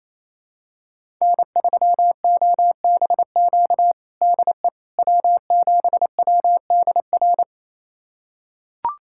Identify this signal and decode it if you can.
Morse code, heard as a keyed tone, about 28 wpm, 700 Hz. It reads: N3OBQ DE W7WDR
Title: USA Callsign DE Callsign - 5 Characters - 28wpm